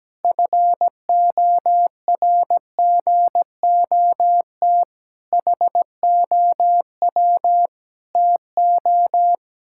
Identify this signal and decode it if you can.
Morse code, heard as a keyed tone, about 17 wpm, 700 Hz. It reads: FORGOT HOW TO
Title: Pangram Challenge v2 - 17wpm